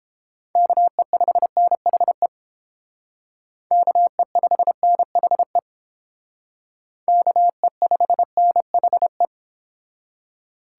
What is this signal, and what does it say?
Morse code, heard as a keyed tone, about 28 wpm, 700 Hz. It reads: KE5NHE KE5NHE KE5NHE